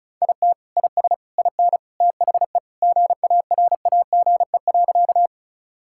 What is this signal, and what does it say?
Morse code, heard as a keyed tone, about 36 wpm, 700 Hz. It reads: IT IS IN THE GARAGE.
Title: Sentences from Top 1600 Words 35wpm